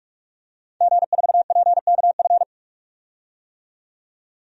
Morse code, heard as a keyed tone, about 46 wpm, 700 Hz. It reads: G4PKF
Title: DXCC-100 Call Signs - 5 Characters - 45wpm